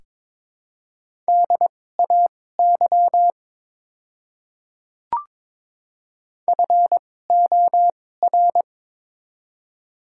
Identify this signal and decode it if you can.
Morse code, heard as a keyed tone, about 23 wpm, spaced as for 16 wpm, 700 Hz. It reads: DAY FOR
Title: 2x Character Spacing - Top 100 Words - 22wpm